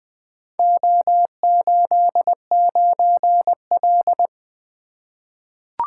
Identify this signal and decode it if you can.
Morse code, heard as a keyed tone, about 20 wpm, 700 Hz. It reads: O89L
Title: Four Letter-Number Combinations - 20wpm